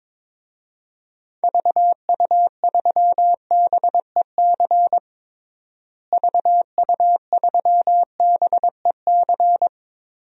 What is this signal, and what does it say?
Morse code, heard as a keyed tone, about 22 wpm, 700 Hz. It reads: VU3BEC VU3BEC
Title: DXCC-100 Call Signs - 6 Characters - Repeated 2x - 22wpm